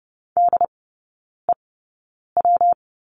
Morse code, heard as a keyed tone, about 30 wpm, spaced as for 9 wpm, 700 Hz. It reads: DEW
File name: Lesson 19e - Letter w - Three Characters 30wpm